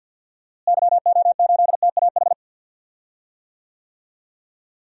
Morse code, heard as a keyed tone, about 47 wpm, 700 Hz. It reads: YO8TAH
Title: DXCC-100 Call Signs - 6 Characters - Repeated 2x - 50wpm